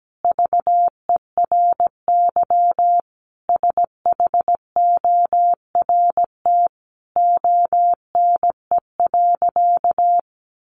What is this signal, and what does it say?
Morse code, heard as a keyed tone, about 17 wpm, 700 Hz. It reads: VERY SHORT ONE.
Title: Sentences from Top 1100 Words 17wpm